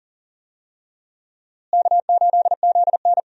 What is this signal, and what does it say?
Morse code, heard as a keyed tone, about 40 wpm, 700 Hz. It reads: K8ZN
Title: USA Call Signs - 4 Characters - Repeated 3x - 40wpm